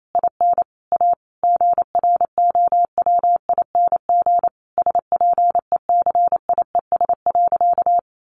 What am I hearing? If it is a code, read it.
Morse code, heard as a keyed tone, about 28 wpm, 700 Hz. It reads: IN A GROWING SPECIES.
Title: A Sound Of Thunder 28wpm